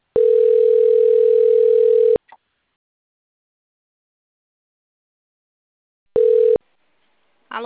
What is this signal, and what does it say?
Ringback tone (North America), 440 + 480 Hz.